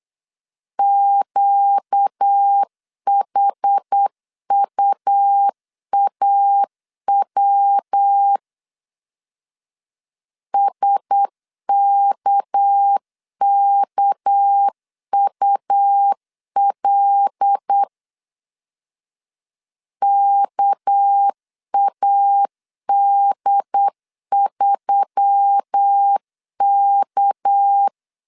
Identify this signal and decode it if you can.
Morse code, heard as a keyed tone, about 8 wpm, 800 Hz. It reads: QHUAW SKKUL KAD3K
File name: LEZIONI / 6 PKX0 / 6 PK 8wpm 3003